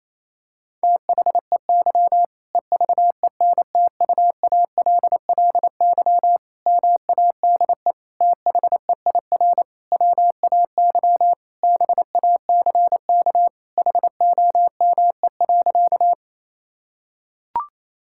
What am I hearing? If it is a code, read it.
Morse code, heard as a keyed tone, about 27 wpm, 700 Hz. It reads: THEY EVENTUALLY MADE THEIR WAY BACK HOME.
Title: Sentences from Top 1000 Words - Review Part 6 - 28wpm